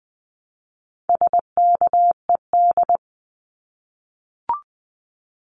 Morse code, heard as a keyed tone, about 20 wpm, 700 Hz. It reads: SKED
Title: Abbreviations Repeated 3x - 20wpm